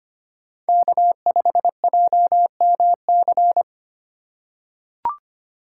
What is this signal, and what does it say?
Morse code, heard as a keyed tone, about 25 wpm, 700 Hz. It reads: K5JMC
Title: USA Call Signs - 5 Characters - 2x Character Spacing - 25wpm